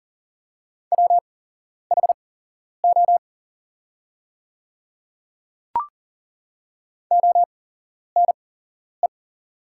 Morse code, heard as a keyed tone, about 40 wpm, spaced as for 11 wpm, 690 Hz. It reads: WHO ONE
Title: 8x Character Spacing - Top 100 Words - 40wpm